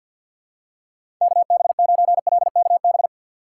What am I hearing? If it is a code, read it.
Morse code, heard as a keyed tone, about 52 wpm, 700 Hz. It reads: KB9LKB
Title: USA Call Signs - 6 Characters - 4x Character Spacing - 50wpm